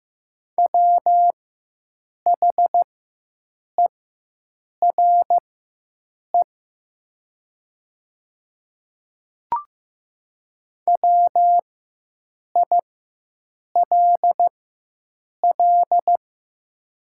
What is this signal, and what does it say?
Morse code, heard as a keyed tone, about 15 wpm, spaced as for 7 wpm, 690 Hz. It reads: WHERE WILL
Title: 4x Character Spacing - Top 500 Words - 15wpm